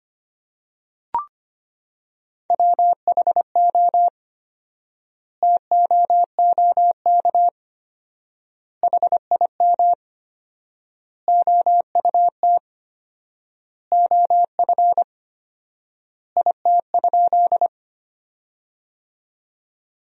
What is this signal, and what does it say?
Morse code, heard as a keyed tone, about 25 wpm, 700 Hz. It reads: WHO TOOK HIM OUT OF IT?
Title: 4x Word-Spacing - Sentences from Top 300 Words - 25wpm